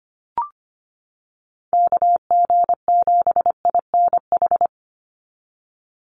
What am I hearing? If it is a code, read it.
Morse code, heard as a keyed tone, about 25 wpm, 700 Hz. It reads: KG7INH
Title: USA Call Signs 25wpm